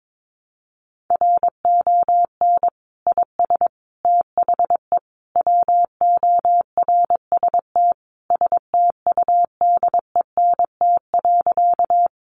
Morse code, heard as a keyed tone, about 22 wpm, 700 Hz. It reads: RON IS THE WORST STUDENT.
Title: Additional Easy Phrases 22wpm